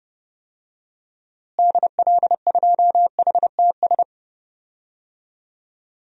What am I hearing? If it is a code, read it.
Morse code, heard as a keyed tone, about 32 wpm, 700 Hz. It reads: DL2HTS
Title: DXCC-100 Call Signs - 6 Characters - Repeated 3x - 30wpm